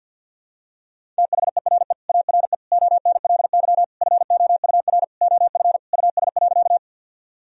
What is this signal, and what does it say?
Morse code, heard as a keyed tone, about 52 wpm, 700 Hz. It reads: THERE ARE ONLY FOUR OF US.
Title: Sentences from Top 1000 Words - Review Part 1 - 50wpm